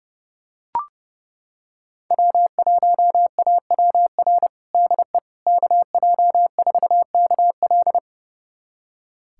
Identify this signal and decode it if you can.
Morse code, heard as a keyed tone, about 30 wpm, 700 Hz. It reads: W1AWR DE KJ4KL
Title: USA Callsign DE Callsign - 5 Characters - 30wpm